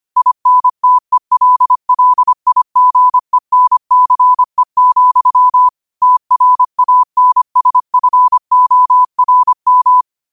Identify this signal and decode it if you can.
Morse code, heard as a keyed tone, about 25 wpm, 1000 Hz. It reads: INTELLIGENCE, TRANSFORM
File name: qotdfiles / 25wpm